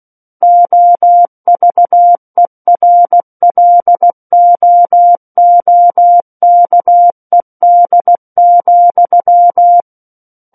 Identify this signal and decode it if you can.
Morse code, heard as a keyed tone, about 16 wpm, 700 Hz. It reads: OVERLOOKED,
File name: Book One / 07-Chapter 7 - 16 WPM